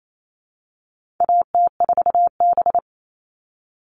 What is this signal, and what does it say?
Morse code, heard as a keyed tone, about 28 wpm, 700 Hz. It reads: AT4B
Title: All Cut Numbers - 4-Digit 28wpm